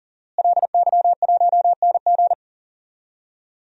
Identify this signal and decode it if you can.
Morse code, heard as a keyed tone, about 40 wpm, 700 Hz. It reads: LY1NG
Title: DXCC-100 Call Signs - 5 Characters - Repeated 3x - 40wpm